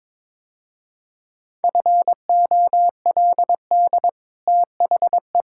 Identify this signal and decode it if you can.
Morse code, heard as a keyed tone, about 22 wpm, 700 Hz. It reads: FOLD THE
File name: practice / 22wpm / Easy Sentences with Hints - 22wpm